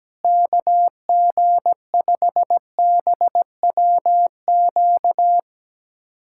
Morse code, heard as a keyed tone, about 17 wpm, 700 Hz. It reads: KG5BWQ
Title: USA Callsign DE Callsign 17wpm